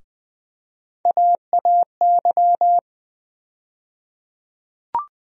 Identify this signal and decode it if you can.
Morse code, heard as a keyed tone, about 20 wpm, 700 Hz. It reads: AAY
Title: USA Call Sign Postfixes - 3 Letters 20wpm